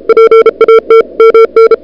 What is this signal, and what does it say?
Morse code, heard as a keyed tone, about 37 wpm, 440 Hz. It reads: PAT MN
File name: n6tr / Fab2007CWSprintHour1